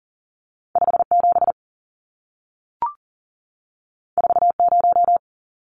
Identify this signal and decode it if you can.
Morse code, heard as a keyed tone, about 40 wpm, 700 Hz. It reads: 57 40
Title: Numbers - 2-Digit 40wpm